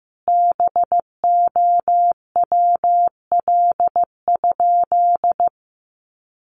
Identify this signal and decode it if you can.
Morse code, heard as a keyed tone, about 15 wpm, 700 Hz. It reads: BOWL?
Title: Sentences from Top 1200 Words 15wpm